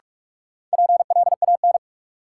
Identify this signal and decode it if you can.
Morse code, heard as a keyed tone, about 43 wpm, 700 Hz. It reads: PLAN
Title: Sentences from Top 500 Words - New Word Drill - 45wpm